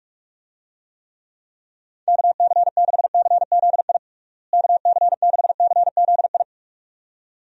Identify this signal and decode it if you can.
Morse code, heard as a keyed tone, about 42 wpm, 700 Hz. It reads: KC6CZI KC6CZI
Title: USA Call Signs - 6 Characters - Repeated 2x - 45wpm